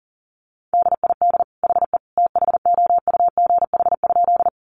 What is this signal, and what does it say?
Morse code, heard as a keyed tone, about 40 wpm, 700 Hz. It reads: DID HE THOUGH?
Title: Sentences from Top 400 Words 40wpm